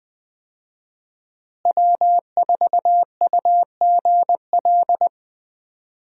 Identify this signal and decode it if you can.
Morse code, heard as a keyed tone, about 20 wpm, 700 Hz. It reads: W4UGL
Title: USA Call Signs - 5 Characters - Repeated 3x - 20wpm